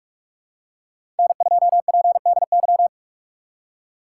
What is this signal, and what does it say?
Morse code, heard as a keyed tone, about 44 wpm, 700 Hz. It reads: N2PDY
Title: USA Call Signs - 5 Characters - 6x Character Spacing - 45wpm